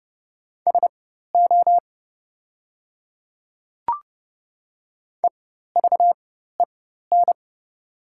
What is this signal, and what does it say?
Morse code, heard as a keyed tone, about 30 wpm, spaced as for 14 wpm, 690 Hz. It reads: SO EVEN
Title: 4x Character Spacing - Top 200 Words - 30wpm